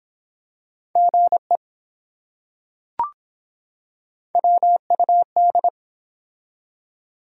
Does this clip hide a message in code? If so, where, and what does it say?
Yes: Morse code from 1.0 to 5.7 s, heard as a keyed tone, about 24 wpm, 700 Hz. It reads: GE WUD